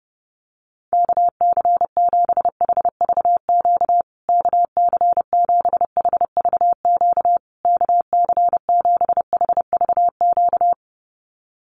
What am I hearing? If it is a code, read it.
Morse code, heard as a keyed tone, about 30 wpm, 700 Hz. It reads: KC7HVQ KC7HVQ KC7HVQ
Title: USA Call Signs Repeated 3x 30wpm